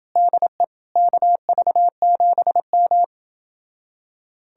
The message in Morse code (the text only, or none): DE KV7M